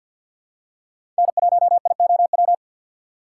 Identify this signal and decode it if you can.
Morse code, heard as a keyed tone, about 53 wpm, 700 Hz. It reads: N1IOW
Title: USA Call Signs - 4x Character Spacing - 50wpm